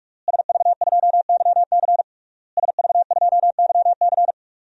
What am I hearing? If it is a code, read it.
Morse code, heard as a keyed tone, about 46 wpm, 700 Hz. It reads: SV2YC SV2YC
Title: DXCC-100 Call Signs - 5 Characters - Repeated 3x - 45wpm